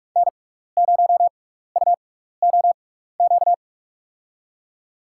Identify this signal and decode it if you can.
Morse code, heard as a keyed tone, about 42 wpm, spaced as for 15 wpm, 700 Hz. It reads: N0UOQ
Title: USA Call Signs - 6x Character Spacing - 45wpm